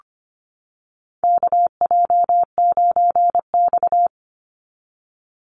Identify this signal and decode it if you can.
Morse code, heard as a keyed tone, about 25 wpm, 700 Hz. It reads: KJ9X
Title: NA SOTA Chaser Callsigns 25wpm